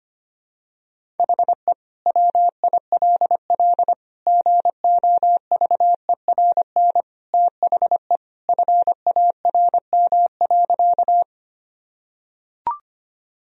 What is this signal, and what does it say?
Morse code, heard as a keyed tone, about 25 wpm, 700 Hz. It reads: HE WILL GOVERN THE FARM.
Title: Sentences from Top 1000 Words - Review Part 2 - 25wpm